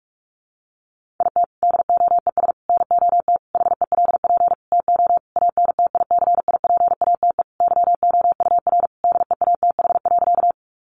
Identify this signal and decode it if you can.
Morse code, heard as a keyed tone, about 44 wpm, 700 Hz. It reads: IT DOES NOT HELP TO ANTICIPATE YOUR DEATH.
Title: Sentences from Top 1500 Words 45wpm